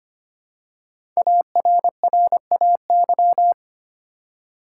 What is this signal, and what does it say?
Morse code, heard as a keyed tone, about 25 wpm, 700 Hz. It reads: ARRAY
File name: 5 Letter Words - First 3 Letters Given - 25wpm